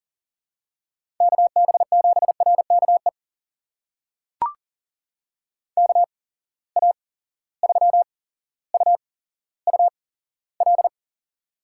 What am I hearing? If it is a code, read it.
Morse code, heard as a keyed tone, about 40 wpm, spaced as for 17 wpm, 700 Hz. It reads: KB7RKE KA3UUL
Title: USA Call Signs - 6 Characters - 8x Character Spacing - 40wpm